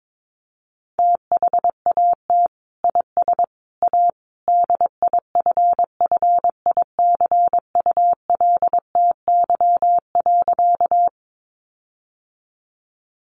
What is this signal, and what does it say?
Morse code, heard as a keyed tone, about 22 wpm, 700 Hz. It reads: THAT IS A DIFFICULTY.
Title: Sentences from Top 1500 Words 22wpm